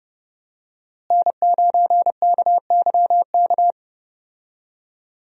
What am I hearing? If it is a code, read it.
Morse code, heard as a keyed tone, about 31 wpm, 700 Hz. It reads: N9KYK
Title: USA Call Signs - 5 Characters - Repeated 2x - 30wpm